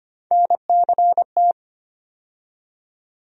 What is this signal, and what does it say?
Morse code, heard as a keyed tone, about 25 wpm, 700 Hz. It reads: NCT
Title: USA Call Sign Postfixes - 3 Letters 25wpm